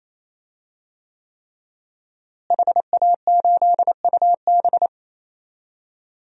Morse code, heard as a keyed tone, about 28 wpm, 700 Hz. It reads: HA8UB
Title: DXCC-100 Call Signs - 5 Characters - Repeated 3x - 28wpm